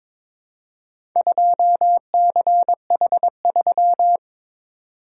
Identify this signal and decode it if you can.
Morse code, heard as a keyed tone, about 23 wpm, 700 Hz. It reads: 2CH3
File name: practice / 22wpm / Four Letter-Number Combinations - 22wpm